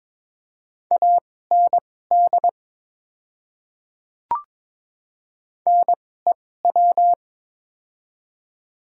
Morse code, heard as a keyed tone, about 22 wpm, spaced as for 16 wpm, 700 Hz. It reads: AND NEW
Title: 2x Character Spacing - Top 200 Words - 22wpm